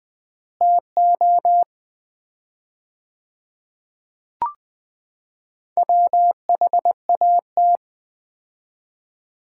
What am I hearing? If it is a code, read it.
Morse code, heard as a keyed tone, about 21 wpm, 700 Hz. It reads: TO WHAT
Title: 4x Word-Spacing - Sets of 5 Words - 20wpm